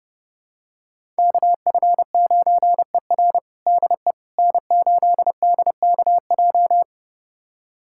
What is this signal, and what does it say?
Morse code, heard as a keyed tone, about 32 wpm, 700 Hz. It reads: KF9ER DE N8DKJ
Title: USA Callsign DE Callsign 30wpm